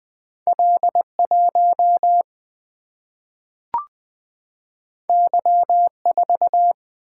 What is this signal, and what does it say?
Morse code, heard as a keyed tone, about 20 wpm, 700 Hz. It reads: L1 Y4
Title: Single-Letter Single-Number 20wpm